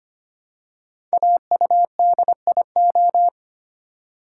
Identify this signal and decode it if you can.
Morse code, heard as a keyed tone, about 25 wpm, 700 Hz. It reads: AUDIO
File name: 5 Letter Words - First 3 Letters Given - 25wpm